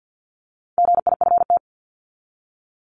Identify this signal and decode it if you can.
Morse code, heard as a keyed tone, about 50 wpm, 700 Hz. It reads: GIFT
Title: Top 751-1000 Words 50wpm